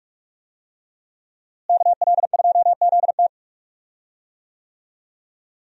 Morse code, heard as a keyed tone, about 45 wpm, 700 Hz. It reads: KL2ZT
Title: USA Call Signs - 5 Characters - Repeated 2x - 45wpm